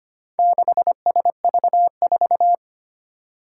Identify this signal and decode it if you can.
Morse code, heard as a keyed tone, about 25 wpm, 700 Hz. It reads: 6SV4
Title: Four Challenging Letter-Number Combinations 25wpm